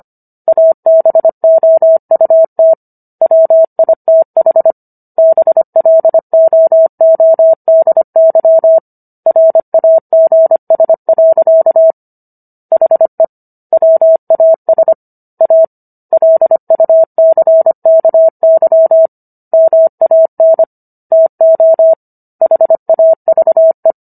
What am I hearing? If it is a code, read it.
Morse code, heard as a keyed tone, about 25 wpm, 650 Hz. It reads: ABOUT WITH BLOODY RAGS. HE WAS A LUCKY MAN TO HAVE